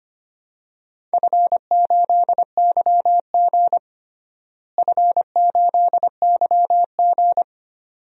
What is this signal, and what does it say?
Morse code, heard as a keyed tone, about 25 wpm, 700 Hz. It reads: F8YG F8YG
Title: DXCC-100 Call Signs - 4 Characters - Repeated 2x - 25wpm